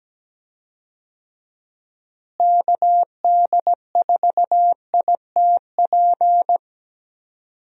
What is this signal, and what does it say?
Morse code, heard as a keyed tone, about 17 wpm, 700 Hz. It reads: KD4ITP
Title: USA Call Signs - 6 Characters - Repeated 2x - 17wpm